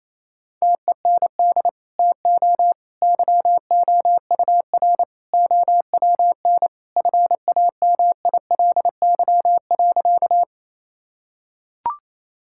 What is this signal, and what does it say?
Morse code, heard as a keyed tone, about 27 wpm, 700 Hz. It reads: TEND TO YOUR OWN FAMILY.
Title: Sentences from Top 1000 Words - Review Part 5 - 28wpm